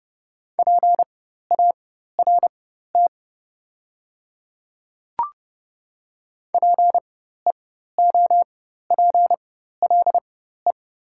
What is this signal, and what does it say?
Morse code, heard as a keyed tone, about 30 wpm, spaced as for 14 wpm, 700 Hz. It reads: PART PEOPLE
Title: 4x Character Spacing - Top 300 Words - 30wpm